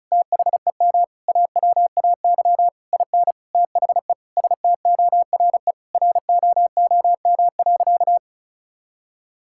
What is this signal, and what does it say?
Morse code, heard as a keyed tone, about 36 wpm, 700 Hz. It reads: THEM AWAY IN THE STORE ROOM.
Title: Sentences from Top 1000 Words - Review Part 5 - 35wpm